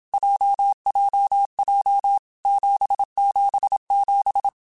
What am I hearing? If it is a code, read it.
Morse code, heard as a keyed tone, about 27 wpm, 800 Hz. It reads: JJJ 777
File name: Groups@25 / Groups@25 mp3s / JZ17@25